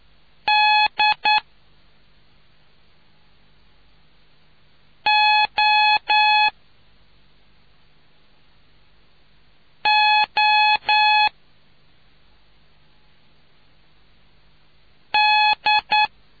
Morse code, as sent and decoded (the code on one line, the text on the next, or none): -.. --- --- -..
DOOD